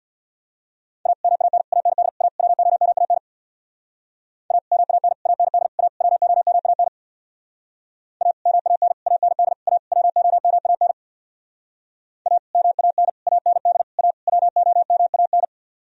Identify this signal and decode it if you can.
Morse code, heard as a keyed tone, about 63 wpm, 700 Hz. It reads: A MAN AND A WOMAN A MAN AND A WOMAN A MAN AND A WOMAN A MAN AND A WOMAN